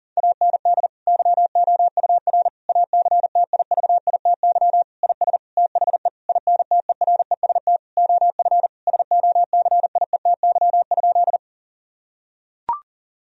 Morse code, heard as a keyed tone, about 37 wpm, 700 Hz. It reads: AND YOUR ACTIVITY IS THE INTEREST OF SOCIETY?
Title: Sentences from Top 1000 Words - Review Part 4 - 40wpm